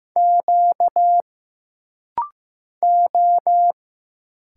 Morse code, heard as a keyed tone, about 15 wpm, 700 Hz. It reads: Q O
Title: Single Letters - Rapid-Fire 15wpm